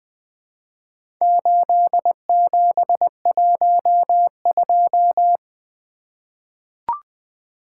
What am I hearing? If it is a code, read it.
Morse code, heard as a keyed tone, about 20 wpm, 700 Hz. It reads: 8712
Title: Four Letter-Number Combinations - 20wpm